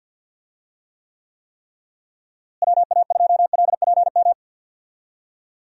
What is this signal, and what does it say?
Morse code, heard as a keyed tone, about 48 wpm, 700 Hz. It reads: WA2LPM
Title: USA Call Signs - 6 Characters - Repeated 3x - 50wpm